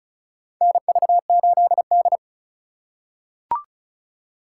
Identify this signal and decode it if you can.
Morse code, heard as a keyed tone, about 36 wpm, 700 Hz. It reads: NV8D
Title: USA Call Signs - 4x Character Spacing - 35wpm